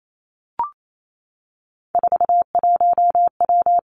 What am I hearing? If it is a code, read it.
Morse code, heard as a keyed tone, about 28 wpm, 700 Hz. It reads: 41W
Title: Three Challenging Letter-Number Combinations 28wpm